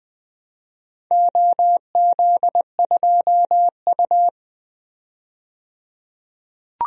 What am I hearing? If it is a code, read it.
Morse code, heard as a keyed tone, about 20 wpm, 700 Hz. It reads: OZ2U
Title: DXCC-100 Call Signs - 4 Characters - 20wpm